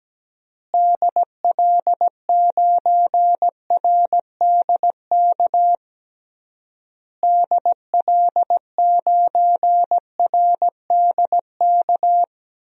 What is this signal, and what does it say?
Morse code, heard as a keyed tone, about 17 wpm, 700 Hz. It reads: DL9RDK DL9RDK
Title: DXCC-100 Call Signs - 6 Characters - Repeated 3x - 17wpm